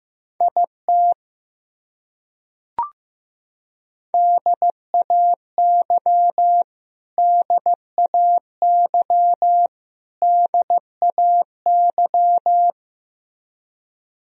Morse code, heard as a keyed tone, about 15 wpm, 700 Hz. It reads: IT DAY DAY DAY
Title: Top 100 Words Repeated 15wpm